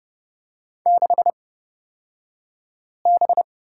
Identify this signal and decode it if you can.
Morse code, heard as a keyed tone, about 30 wpm, spaced as for 5 wpm, 700 Hz. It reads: 6B
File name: Lesson 36d - BT - Two Characters 30wpm